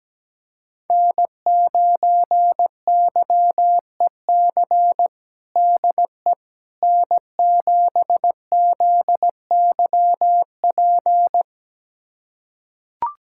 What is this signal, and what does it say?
Morse code, heard as a keyed tone, about 17 wpm, 700 Hz. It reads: N9YEC DE N7ZYP
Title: USA Callsign DE Callsign - 5 Characters - 17wpm